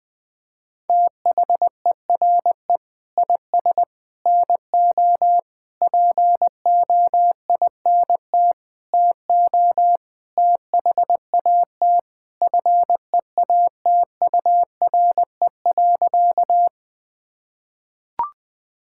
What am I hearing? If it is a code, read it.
Morse code, heard as a keyed tone, about 20 wpm, 700 Hz. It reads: THERE IS NO POINT TO THAT FEATURE.
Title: Sentences from Top 1000 Words - Review Part 5 - 20wpm